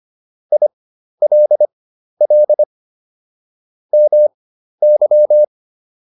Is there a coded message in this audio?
Yes: Morse code from 0.5 to 5.4 s, heard as a keyed tone, about 25 wpm, spaced as for 12 wpm, 600 Hz. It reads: ILL MY